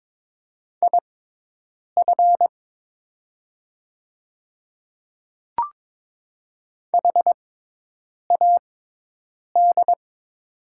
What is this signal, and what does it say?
Morse code, heard as a keyed tone, about 22 wpm, spaced as for 8 wpm, 700 Hz. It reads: IF HAD
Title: 6x Character Spacing - Top 100 Words - 22wpm